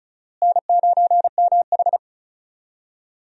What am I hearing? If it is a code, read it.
Morse code, heard as a keyed tone, about 36 wpm, 700 Hz. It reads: N9MH
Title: USA Call Signs - 4 Characters - Repeated 3x - 35wpm